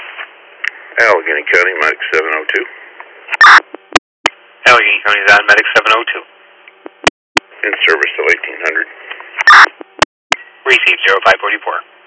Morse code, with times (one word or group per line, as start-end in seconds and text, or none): none